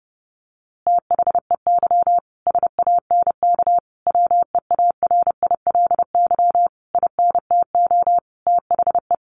Morse code, heard as a keyed tone, about 30 wpm, 700 Hz. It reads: THEY SANK WEARILY INTO THE
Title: A Sound Of Thunder 30wpm